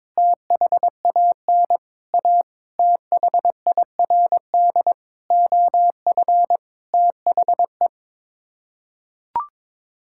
Morse code, heard as a keyed tone, about 22 wpm, 700 Hz. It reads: THAN A THIRD OF THE
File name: practice / 22wpm / Sets of 6 Words - 22wpm